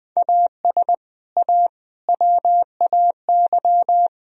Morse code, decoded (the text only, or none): AS A WAY